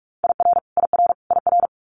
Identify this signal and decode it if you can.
Morse code, heard as a keyed tone, about 46 wpm, 700 Hz. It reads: IR IR IR
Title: Multi-letter Phonemes Repeated 3x 45wpm